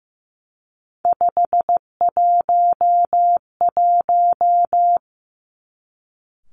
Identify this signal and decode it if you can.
Morse code, heard as a keyed tone, about 15 wpm, 700 Hz. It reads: 511
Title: Three Challenging Letter-Number Combinations 15wpm